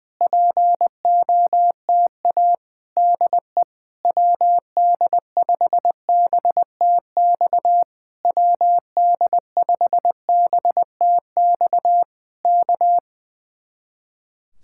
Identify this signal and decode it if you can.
Morse code, heard as a keyed tone, about 20 wpm, 700 Hz. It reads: POTA DE WD5BTX WD5BTX K
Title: Speed-Racing Common QSO Phrases - 20wpm